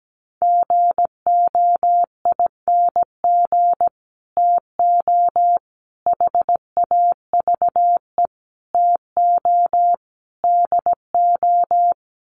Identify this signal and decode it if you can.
Morse code, heard as a keyed tone, about 17 wpm, 690 Hz. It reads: GOING TO HAVE TO DO
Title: Sets of 5 Words Encore 17wpm